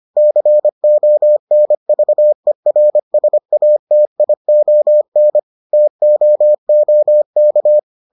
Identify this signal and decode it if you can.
Morse code, heard as a keyed tone, about 25 wpm, 600 Hz. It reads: CONVERSATION TOOK